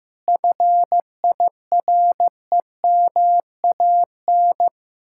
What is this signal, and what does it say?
Morse code, heard as a keyed tone, about 15 wpm, 700 Hz. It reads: FIREMAN
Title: Speed-Racing Common QSO Phrases Encore - 15wpm